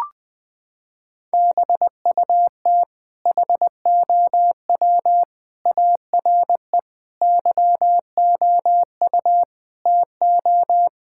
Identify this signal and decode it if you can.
Morse code, heard as a keyed tone, about 21 wpm, 700 Hz. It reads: BUT HOW ARE YOU TO
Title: Sentences from Top 1000 Words - Review Part 6 - 20wpm